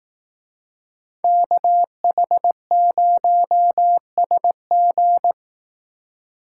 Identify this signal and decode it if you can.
Morse code, heard as a keyed tone, about 18 wpm, 700 Hz. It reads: KH0SG
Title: Speed Racing - USA 2x2 Call Signs 15wpm